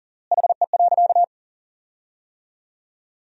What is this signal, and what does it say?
Morse code, heard as a keyed tone, about 40 wpm, 700 Hz. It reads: HE.
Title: 3x Word-Spacing - Sentences from Top 400 Words - 40wpm